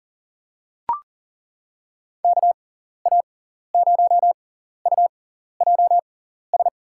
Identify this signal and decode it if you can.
Morse code, heard as a keyed tone, about 38 wpm, spaced as for 14 wpm, 700 Hz. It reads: KA0UJS
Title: USA Call Signs - 6 Characters - 6x Character Spacing - 40wpm